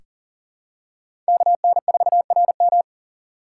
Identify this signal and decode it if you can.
Morse code, heard as a keyed tone, about 39 wpm, 690 Hz. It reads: KN4RM
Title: USA Call Signs - 4x Character Spacing - 40wpm